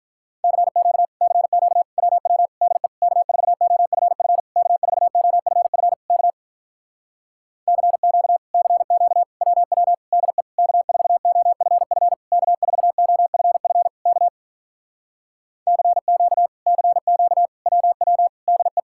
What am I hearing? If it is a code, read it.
Morse code, heard as a keyed tone, about 46 wpm, 700 Hz. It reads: CQ CQ WW DE K4OFF K4OFF K CQ CQ WW DE K4OFF K4OFF K CQ CQ WW DE